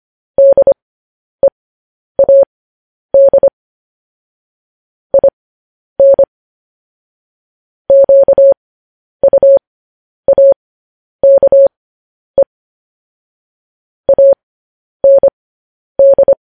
Morse code, heard as a keyed tone, about 25 wpm, spaced as for 10 wpm, 550 Hz. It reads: DEAD IN QUAKE AND
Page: news-headlines-in-morse-code-at-10-wpm